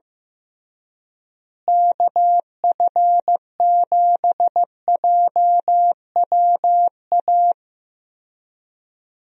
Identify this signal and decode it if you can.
Morse code, heard as a keyed tone, about 15 wpm, 700 Hz. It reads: KF7JWA
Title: USA Call Signs - 6 Characters - Repeated 2x - 15wpm